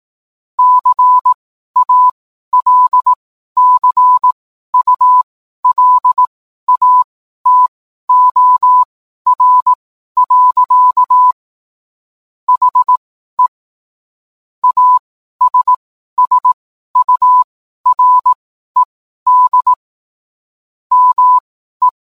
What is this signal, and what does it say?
Morse code, heard as a keyed tone, about 18 wpm, spaced as for 12 wpm, 1000 Hz. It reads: CALCULATOR. HE ASSURED ME